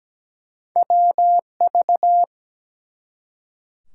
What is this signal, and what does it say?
Morse code, heard as a keyed tone, about 17 wpm, 700 Hz. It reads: WV
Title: US States and Canadian Provinces 17wpm